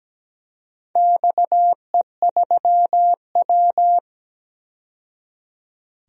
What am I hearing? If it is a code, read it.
Morse code, heard as a keyed tone, about 17 wpm, 700 Hz. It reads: XE3W
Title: DXCC-100 Call Signs - 4 Characters - 17wpm